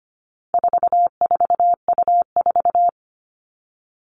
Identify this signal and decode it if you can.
Morse code, heard as a keyed tone, about 25 wpm, 700 Hz. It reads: 44U4
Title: All Cut Numbers - 4-Digit 25wpm